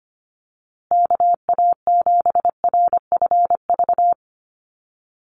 Morse code, heard as a keyed tone, about 25 wpm, 700 Hz. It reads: KA7RFV
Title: USA Call Signs Repeated 2x 25wpm